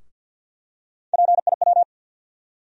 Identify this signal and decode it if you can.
Morse code, heard as a keyed tone, about 53 wpm, 700 Hz. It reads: PIW
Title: USA Call Sign Postfixes - 3 Letters 50wpm